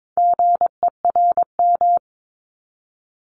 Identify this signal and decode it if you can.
Morse code, heard as a keyed tone, about 22 wpm, 700 Hz. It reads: GERM
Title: Multi-letter Phonemes 22wpm